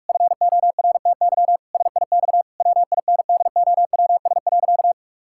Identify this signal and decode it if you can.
Morse code, heard as a keyed tone, about 41 wpm, 700 Hz. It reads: FORTY SIX WINDOWS.
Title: Pangram Challenge v2 - 45wpm